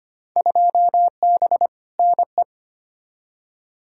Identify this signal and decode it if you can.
Morse code, heard as a keyed tone, about 25 wpm, 700 Hz. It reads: 2B NE
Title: Field Day Exchanges Encore - 25wpm